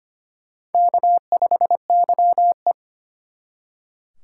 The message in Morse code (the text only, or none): K5YE